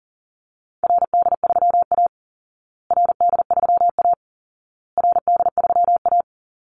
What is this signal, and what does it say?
Morse code, heard as a keyed tone, about 40 wpm, 700 Hz. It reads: RD3A RD3A RD3A
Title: Call Signs Repeated 2x - 40wpm